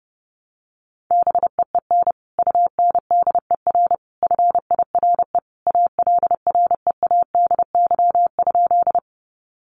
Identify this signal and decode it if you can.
Morse code, heard as a keyed tone, about 30 wpm, 700 Hz. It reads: BEEN UNDER FIRE ALREADY?
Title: Sentences from Top 700 Words 30wpm